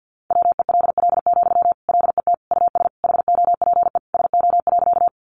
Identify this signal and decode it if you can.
Morse code, heard as a keyed tone, about 50 wpm, 700 Hz. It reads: WELL, LET US HOPE SO.
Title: Sentences from Top 1000 Words 50wpm